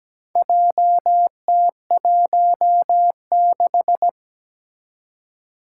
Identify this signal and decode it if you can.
Morse code, heard as a keyed tone, about 17 wpm, 700 Hz. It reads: JT16
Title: Four Challenging Letter-Number Combinations 17wpm